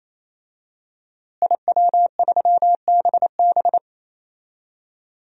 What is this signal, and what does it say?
Morse code, heard as a keyed tone, about 28 wpm, 700 Hz. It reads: IW3BB
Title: DXCC-100 Call Signs - 5 Characters - Repeated 3x - 28wpm